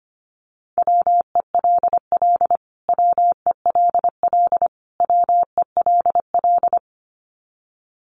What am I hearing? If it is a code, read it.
Morse code, heard as a keyed tone, about 25 wpm, 700 Hz. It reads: WELL WELL WELL
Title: Top 500 Words Repeated 25wpm